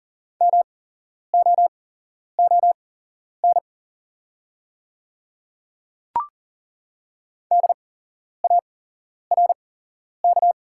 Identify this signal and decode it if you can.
Morse code, heard as a keyed tone, about 41 wpm, spaced as for 11 wpm, 700 Hz. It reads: MOON DARK
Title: 8x Character Spacing - Top 500 Words - 40wpm